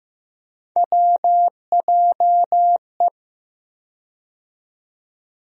Morse code, heard as a keyed tone, about 15 wpm, 700 Hz. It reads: WJE
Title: USA Call Sign Postfixes - 3 Letters 15wpm